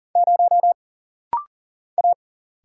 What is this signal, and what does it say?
Morse code, heard as a keyed tone, about 40 wpm, 700 Hz. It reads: 0 A
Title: Single Letter-Number - Rapid-Fire 40wpm